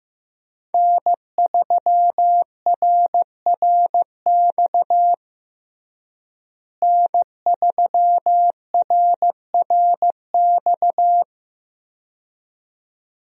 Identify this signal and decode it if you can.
Morse code, heard as a keyed tone, about 15 wpm, 700 Hz. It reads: N3RRX N3RRX
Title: USA Call Signs - 5 Characters - Repeated 2x - 15wpm